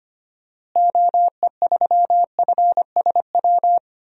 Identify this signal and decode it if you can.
Morse code, heard as a keyed tone, about 25 wpm, 690 Hz. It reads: OE3FSW
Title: DXCC-100 Call Signs - 6 Characters - Repeated 3x - 25wpm